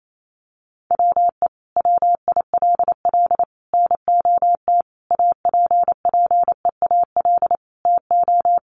Morse code, heard as a keyed tone, about 28 wpm, 700 Hz. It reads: WE WILL NOT APPEAL TO
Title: Sentences from Top 1300 Words 28wpm